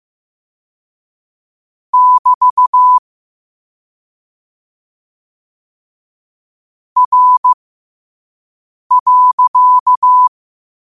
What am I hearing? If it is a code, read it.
Morse code, heard as a keyed tone, about 15 wpm, spaced as for 5 wpm, 1000 Hz. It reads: <BT> R.
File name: qotdfiles / 5wpm